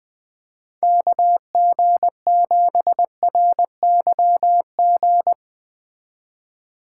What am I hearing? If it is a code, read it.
Morse code, heard as a keyed tone, about 22 wpm, 700 Hz. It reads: KG7RYG
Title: Speed Racing - USA Call Signs 20wpm